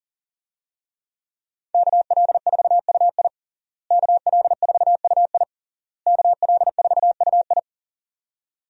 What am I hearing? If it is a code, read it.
Morse code, heard as a keyed tone, about 40 wpm, 700 Hz. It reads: KL4UI KL4UI KL4UI
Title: USA Call Signs - 5 Characters - Repeated 3x - 40wpm